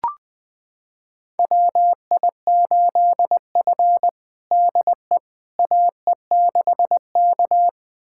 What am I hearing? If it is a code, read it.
Morse code, heard as a keyed tone, about 21 wpm, 700 Hz. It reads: WI8F DE AE6K
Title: USA Callsign DE Callsign - 4 Characters - 20wpm